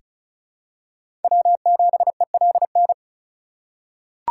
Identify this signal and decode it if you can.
Morse code, heard as a keyed tone, about 34 wpm, 700 Hz. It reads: W7ELN
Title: USA Call Signs - 3x Character Spacing - 35wpm